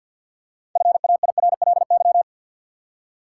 Morse code, heard as a keyed tone, about 47 wpm, 700 Hz. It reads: FAIRLY
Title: Six Letter Words - 50wpm